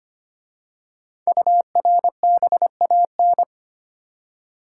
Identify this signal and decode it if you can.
Morse code, heard as a keyed tone, about 25 wpm, 700 Hz. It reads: URBAN
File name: 5 Letter Words - First 3 Letters Given - 25wpm